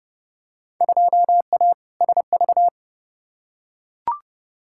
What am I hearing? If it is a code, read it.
Morse code, heard as a keyed tone, about 30 wpm, 700 Hz. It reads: 2A SV
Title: Field Day Exchanges - 30wpm